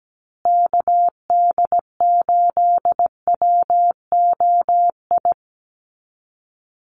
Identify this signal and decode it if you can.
Morse code, heard as a keyed tone, about 17 wpm, 700 Hz. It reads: KD8WOI
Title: USA Call Signs Repeated 2x 17wpm